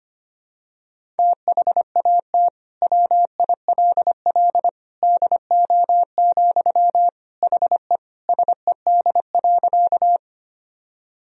Morse code, heard as a keyed tone, about 25 wpm, 700 Hz. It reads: THAT WILL DO, HE SED.
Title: Sentences from Top 100 Words and Abbreviations 25wpm